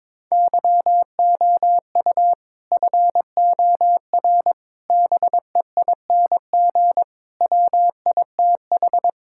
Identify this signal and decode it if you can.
Morse code, heard as a keyed tone, about 22 wpm, 700 Hz. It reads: YOU FOR BEING WITH
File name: practice / 22wpm / Build A Phrase - 5-Word Phrases - 22wpm